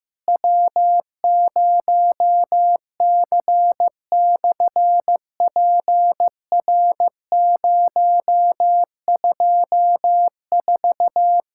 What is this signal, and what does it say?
Morse code, heard as a keyed tone, about 15 wpm, 700 Hz. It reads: W0C/PR024
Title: SOTA QSO Practice 15wpm